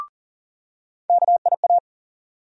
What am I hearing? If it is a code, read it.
Morse code, heard as a keyed tone, about 40 wpm, 700 Hz. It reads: KIA
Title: USA Call Sign Postfixes - 3 Letters 40wpm